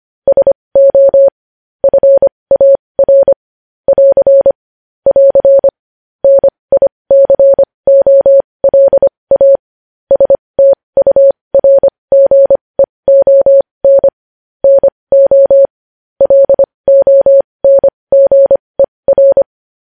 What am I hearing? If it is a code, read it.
Morse code, heard as a keyed tone, about 25 wpm, spaced as for 20 wpm, 550 Hz. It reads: SO FAR <AR> <AR> NICOLA STURGEON NO LONGER
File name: podcast / News headlines-1x-25x20